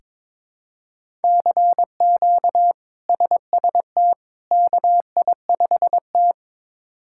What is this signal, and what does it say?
Morse code, heard as a keyed tone, about 22 wpm, 700 Hz. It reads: CQ SST KI5T
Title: K1USN SST QSO Practice - 22wpm